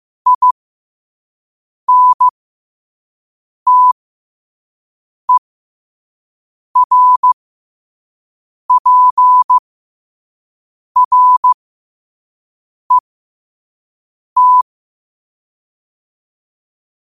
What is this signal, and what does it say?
Morse code, heard as a keyed tone, about 15 wpm, spaced as for 5 wpm, 1000 Hz. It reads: INTERPRET
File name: 5wpm